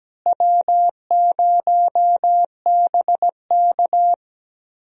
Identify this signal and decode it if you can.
Morse code, heard as a keyed tone, about 17 wpm, 700 Hz. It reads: W0BK
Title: USA Call Signs - 4 Characters - Repeated 2x - 17wpm